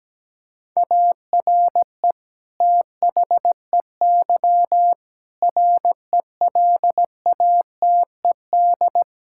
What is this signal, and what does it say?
Morse code, heard as a keyed tone, about 17 wpm, 700 Hz. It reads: ARE THEY RELATED
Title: Sentences from Top 1000 Words - Review Part 5 - 17wpm